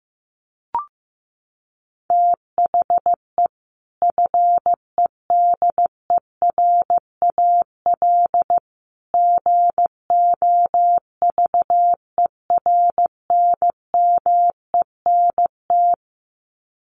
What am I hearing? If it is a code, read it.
Morse code, heard as a keyed tone, about 15 wpm, 700 Hz. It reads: THE FEDERAL GOVERNMENT
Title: Sets of 3 Words 15wpm